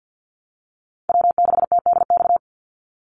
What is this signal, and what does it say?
Morse code, heard as a keyed tone, about 48 wpm, 700 Hz. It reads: W6TDX
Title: NA SOTA Chaser Callsigns 50wpm